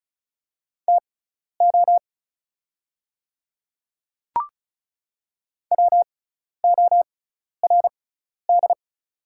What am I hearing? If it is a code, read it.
Morse code, heard as a keyed tone, about 34 wpm, spaced as for 12 wpm, 690 Hz. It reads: TO WORD
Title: 6x Character Spacing - Top 200 Words - 35wpm